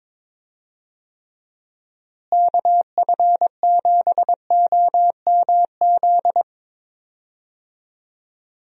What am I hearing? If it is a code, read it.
Morse code, heard as a keyed tone, about 22 wpm, 700 Hz. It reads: KF7OMZ
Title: USA Call Signs - 6 Characters - Repeated 2x - 22wpm